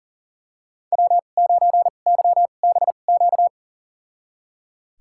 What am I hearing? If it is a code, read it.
Morse code, heard as a keyed tone, about 38 wpm, spaced as for 28 wpm, 700 Hz. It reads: W9YBQ
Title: USA Call Signs - 2x Character Spacing - 40wpm